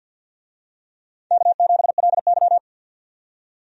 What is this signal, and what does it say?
Morse code, heard as a keyed tone, about 51 wpm, 700 Hz. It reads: K7LY
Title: USA Call Signs - 4 Characters - 50wpm